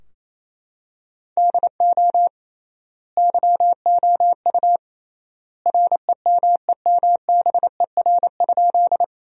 Morse code, heard as a keyed tone, about 28 wpm, 700 Hz. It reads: DO YOU REMEMBER?
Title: 3x Word-Spacing - Sentences from Top 400 Words - 28wpm